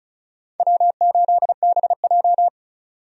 Morse code, heard as a keyed tone, about 34 wpm, 700 Hz. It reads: W8BJ
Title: USA Call Signs - 4 Characters - 35wpm